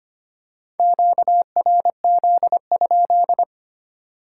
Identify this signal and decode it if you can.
Morse code, heard as a keyed tone, about 25 wpm, 690 Hz. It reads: QRZ?
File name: Top 90 QSO Elements - 3x Character Spacing - 25wpm